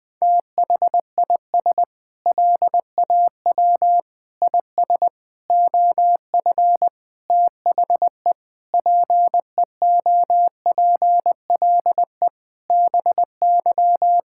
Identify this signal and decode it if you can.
Morse code, heard as a keyed tone, about 20 wpm, 700 Hz. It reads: THIS LAW IS OF THE PEOPLE BY
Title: Sentences with 9-Word Phrases - Spoken Beginning - 20wpm